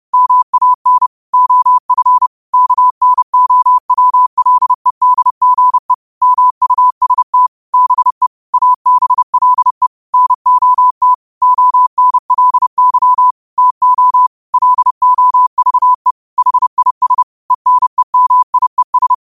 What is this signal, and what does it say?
Morse code, heard as a keyed tone, about 30 wpm, 1000 Hz. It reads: MAN OF KNOWLEDGE MUST BE ABLE NOT ONLY TO LOVE HIS ENEMIES